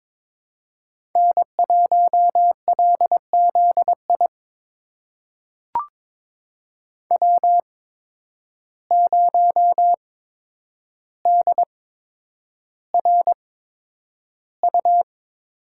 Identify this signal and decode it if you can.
Morse code, heard as a keyed tone, about 22 wpm, spaced as for 9 wpm, 700 Hz. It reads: N1LZI W0DRU
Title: USA Call Signs - 8x Character Spacing - 22wpm